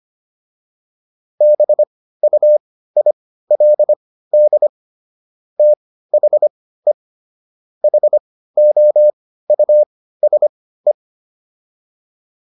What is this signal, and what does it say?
Morse code, heard as a keyed tone, about 25 wpm, spaced as for 15 wpm, 600 Hz. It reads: BUILD THE HOUSE